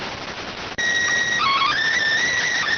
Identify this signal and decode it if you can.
SSTV (slow-scan television), PD 290.